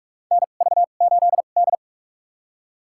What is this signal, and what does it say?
Morse code, heard as a keyed tone, about 44 wpm, spaced as for 32 wpm, 700 Hz. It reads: NV8D
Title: USA Call Signs - 4 Characters - 2x Character Spacing - 45wpm